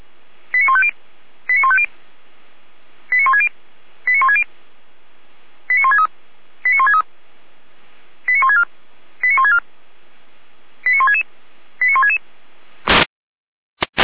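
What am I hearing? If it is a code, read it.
Five-tone selective call (ZVEI), heard as quick runs of five tones, tones of about 70 ms. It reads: 89150 89150 89160 89160 89163 89163 89164 89164 89166 89166
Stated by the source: ZVEI five-tone selective call received on an SDR.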